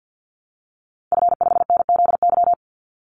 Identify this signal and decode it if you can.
Morse code, heard as a keyed tone, about 50 wpm, 700 Hz. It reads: F5NZY
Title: Call Signs Repeated 3x - 50wpm